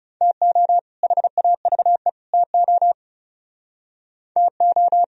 Morse code, heard as a keyed tone, about 36 wpm, 700 Hz. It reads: TO HAVE TO TO
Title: Speed-Racing - Sets of 3 Words 30wpm